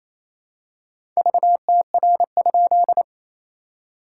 Morse code, heard as a keyed tone, about 28 wpm, 700 Hz. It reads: VTR?
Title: SOTA QSO Practice 28wpm